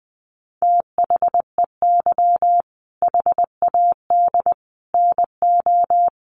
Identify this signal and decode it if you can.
Morse code, heard as a keyed tone, about 20 wpm, 700 Hz. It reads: THEY HAD NO
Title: Sentences from Top 1500 Words 20wpm